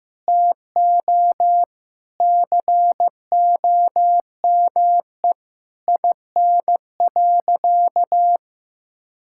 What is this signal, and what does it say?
Morse code, heard as a keyed tone, about 15 wpm, 700 Hz. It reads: TO COME IN.
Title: Sentences from Top 1000 Words - Review Part 5 - 15wpm